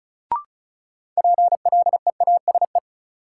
Morse code, heard as a keyed tone, about 36 wpm, 700 Hz. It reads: PLEASE
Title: Sentences from Top 1500 Words - New Word Drill - 35wpm